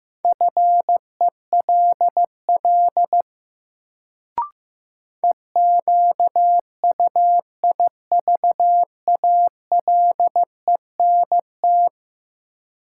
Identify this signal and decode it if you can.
Morse code, heard as a keyed tone, about 15 wpm, 700 Hz. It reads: FELL EQUIVALENT
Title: Sentences from Top 1600 Words - New Word Drill Alt - 15wpm